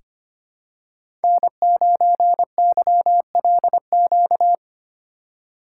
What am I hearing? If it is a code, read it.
Morse code, heard as a keyed tone, about 25 wpm, 700 Hz. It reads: N9YLQ
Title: USA Call Signs - 5 Characters - 8x Character Spacing - 25wpm